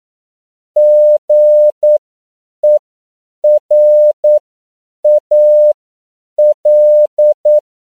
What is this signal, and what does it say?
Morse code, heard as a keyed tone, about 9 wpm, spaced as for 7 wpm, 600 Hz. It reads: GERAL